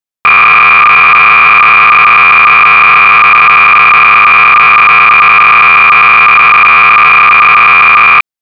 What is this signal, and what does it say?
Cassette data (Kansas City type).